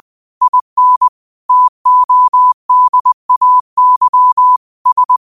Morse code, heard as a keyed tone, about 20 wpm, 1000 Hz. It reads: IN TODAY S
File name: qotdfiles / 20wpm